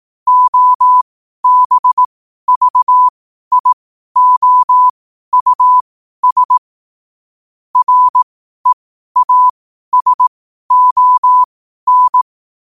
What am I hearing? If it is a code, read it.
Morse code, heard as a keyed tone, about 18 wpm, spaced as for 12 wpm, 1000 Hz. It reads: OBVIOUS REASON
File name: qotdfiles / 12wpm